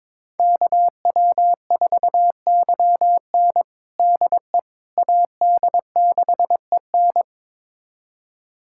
Morse code, heard as a keyed tone, about 22 wpm, 700 Hz. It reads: KW4YN DE AD6EN
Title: USA Callsign DE Callsign - 5 Characters - 22wpm